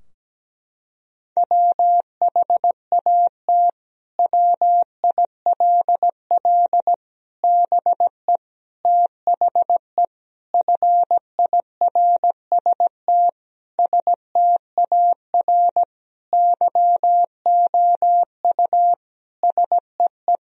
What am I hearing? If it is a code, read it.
Morse code, heard as a keyed tone, about 17 wpm, 700 Hz. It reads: WHAT WILL BE THE FIRST STAR YOU SEE
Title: Sentences from Top 1000 Words - Review Part 2 - 17wpm